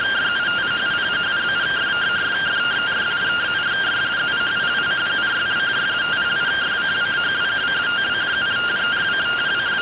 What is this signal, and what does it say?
Two-tone teleprinter signal, about 170 Hz shift.